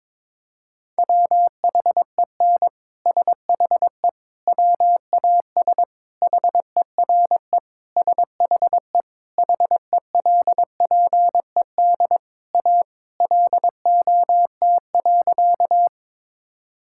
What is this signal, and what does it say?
Morse code, heard as a keyed tone, about 22 wpm, 700 Hz. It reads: WHEN SHE WAS HERE SHE HELPED A LOT.
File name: practice / 22wpm / Sentences with 3-Word Phrases - 22wpm